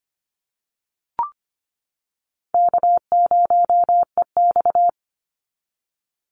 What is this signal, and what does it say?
Morse code, heard as a keyed tone, about 25 wpm, 700 Hz. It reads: K0EX
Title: Easy Call Signs - 25wpm